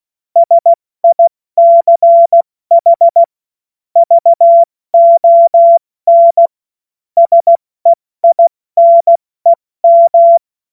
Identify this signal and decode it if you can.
Morse code, heard as a keyed tone, about 16 wpm, 680 Hz. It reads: SICH VON SEINEM
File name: Morsekurs / Witze / Witz 03 Juni Mai 2025 70 F80